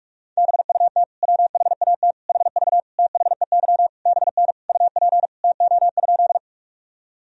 Morse code, heard as a keyed tone, about 44 wpm, 700 Hz. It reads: BUT WHAT HV THEY BN UP TO?
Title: Sentences from Top 100 Words and Abbreviations 45wpm